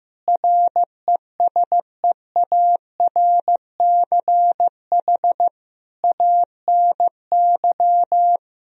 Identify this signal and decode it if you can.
Morse code, heard as a keyed tone, about 15 wpm, 700 Hz. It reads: RESEARCH ANY
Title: Sentences from Top 1000 Words - Review Part 4 - 15wpm